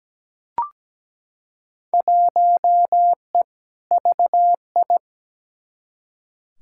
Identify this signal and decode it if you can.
Morse code, heard as a keyed tone, about 17 wpm, 700 Hz. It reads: 1E VI
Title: Field Day Exchanges - 17wpm